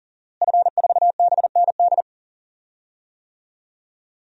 Morse code, heard as a keyed tone, about 40 wpm, 700 Hz. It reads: F4BND
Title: DXCC-100 Call Signs - 5 Characters - Repeated 3x - 40wpm